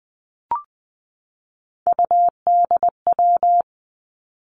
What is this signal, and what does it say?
Morse code, heard as a keyed tone, about 20 wpm, 700 Hz. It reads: UDW
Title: Three Challenging Letter-Number Combinations 20wpm